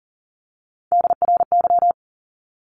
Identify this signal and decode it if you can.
Morse code, heard as a keyed tone, about 40 wpm, 700 Hz. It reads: DRY
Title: Top 500 Words Repeated 40wpm